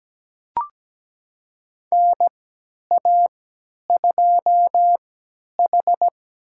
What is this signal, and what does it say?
Morse code, heard as a keyed tone, about 17 wpm, spaced as for 10 wpm, 700 Hz. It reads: NA2H
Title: USA Call Signs - 4 Characters - 3x Character Spacing - 17wpm